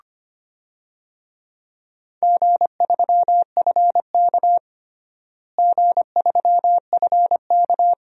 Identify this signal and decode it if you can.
Morse code, heard as a keyed tone, about 25 wpm, 700 Hz. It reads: G3FK G3FK
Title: DXCC-100 Call Signs - 4 Characters - Repeated 2x - 25wpm